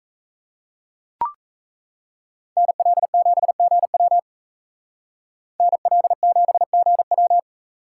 Morse code, heard as a keyed tone, about 35 wpm, 700 Hz. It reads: NL7GW NL7GW